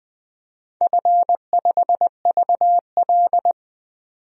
Morse code, heard as a keyed tone, about 20 wpm, 700 Hz. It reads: F5VL
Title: DXCC-100 Call Signs - 4 Characters - 20wpm